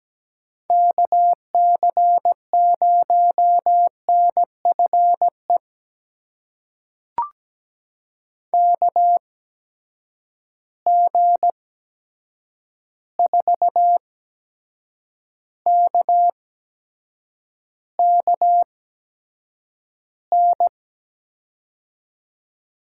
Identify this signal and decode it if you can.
Morse code, heard as a keyed tone, about 17 wpm, spaced as for 7 wpm, 700 Hz. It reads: KC0NFE KG4KKN
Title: USA Call Signs - 6 Characters - 8x Character Spacing - 17wpm